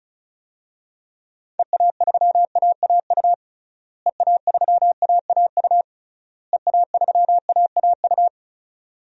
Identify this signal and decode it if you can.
Morse code, heard as a keyed tone, about 36 wpm, 700 Hz. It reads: EA3AAU EA3AAU EA3AAU
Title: DXCC-100 Call Signs - 6 Characters - Repeated 3x - 35wpm